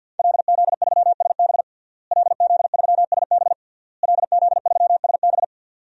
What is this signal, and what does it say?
Morse code, heard as a keyed tone, about 49 wpm, 700 Hz. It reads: LZ3SB LZ3SB LZ3SB
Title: DXCC-100 Call Signs - 5 Characters - Repeated 3x - 50wpm